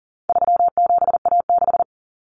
Morse code, heard as a keyed tone, about 40 wpm, 700 Hz. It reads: 37A6
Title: Common Cut Numbers - 4-Digit 40wpm